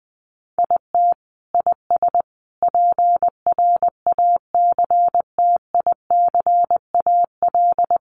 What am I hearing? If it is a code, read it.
Morse code, heard as a keyed tone, about 20 wpm, 700 Hz. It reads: IT IS PRACTICAL
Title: Sentences from Top 1400 Words 20wpm